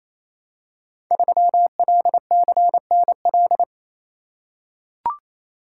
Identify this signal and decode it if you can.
Morse code, heard as a keyed tone, about 28 wpm, 700 Hz. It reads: 3LCNL
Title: Code Groups - 5 Letter:Numbers - 28wpm